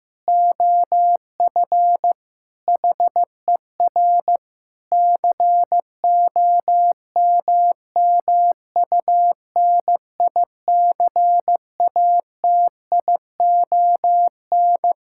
Morse code, heard as a keyed tone, about 15 wpm, 700 Hz. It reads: OF HER COMMUNICATION
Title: Sentences from Top 1000 Words - Review Part 6 - 15wpm